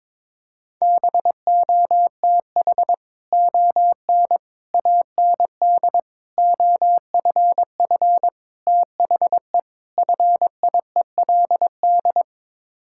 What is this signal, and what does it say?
Morse code, heard as a keyed tone, about 23 wpm, 700 Hz. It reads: BOTH ON AND OFF THE FIELD
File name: practice / 22wpm / Sets of 6 Words - 22wpm